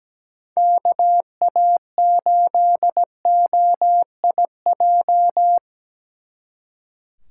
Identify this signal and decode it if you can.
Morse code, heard as a keyed tone, about 20 wpm, 700 Hz. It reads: KA8OIJ
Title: Speed Racing - USA 2x3 Call Signs 17wpm